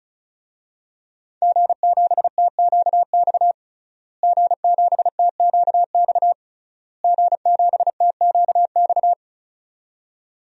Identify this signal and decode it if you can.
Morse code, heard as a keyed tone, about 36 wpm, 700 Hz. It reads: G7TQX G7TQX G7TQX
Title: DXCC-100 Call Signs - 5 Characters - Repeated 3x - 35wpm